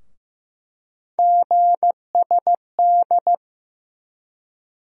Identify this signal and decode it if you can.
Morse code, heard as a keyed tone, about 15 wpm, 700 Hz. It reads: GSD
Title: USA Call Sign Postfixes - 3 Letters 15wpm